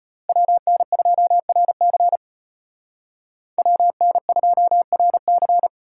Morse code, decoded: WN2RC WN2RC

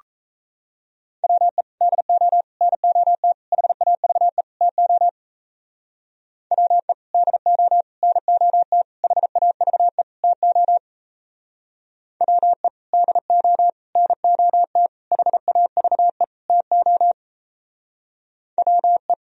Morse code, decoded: WE DO NOT HAVE TO WE DO NOT HAVE TO WE DO NOT HAVE TO WE

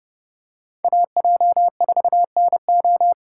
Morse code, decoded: AJ4NO

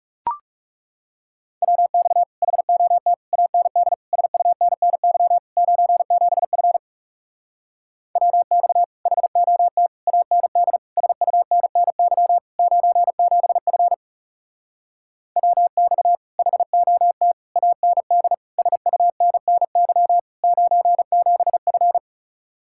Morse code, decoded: WX HOT AND SUNNY 97F WX HOT AND SUNNY 97F WX HOT AND SUNNY 97F

